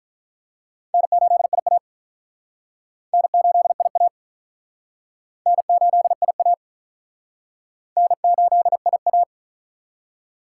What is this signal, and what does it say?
Morse code, heard as a keyed tone, about 44 wpm, 700 Hz. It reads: N8IA N8IA N8IA N8IA